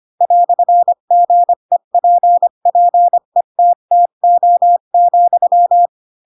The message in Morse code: "GEPPETTO,

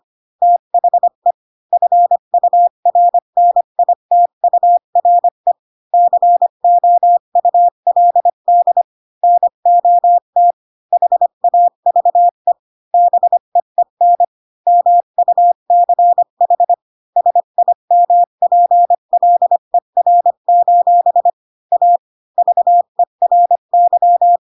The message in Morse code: THE FURNITURE COULD NOT HAVE BEEN MUCH SIMPLER: A VERY

